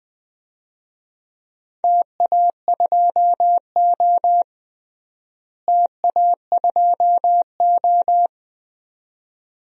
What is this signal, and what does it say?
Morse code, heard as a keyed tone, about 21 wpm, 700 Hz. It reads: TA2O TA2O
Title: DXCC-100 Call Signs - 4 Characters - Repeated 2x - 20wpm